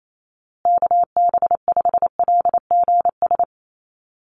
Morse code, heard as a keyed tone, about 28 wpm, 700 Hz. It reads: KB5LGS
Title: USA Call Signs Repeated 3x 28wpm